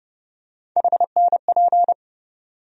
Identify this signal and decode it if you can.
Morse code, heard as a keyed tone, about 30 wpm, 700 Hz. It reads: HNP
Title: USA Call Sign Postfixes - 3 Letters 30wpm